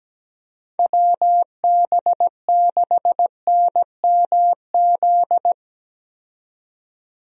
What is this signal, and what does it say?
Morse code, heard as a keyed tone, about 17 wpm, 700 Hz. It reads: WB6NMZ
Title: USA Call Signs - 6 Characters - 17wpm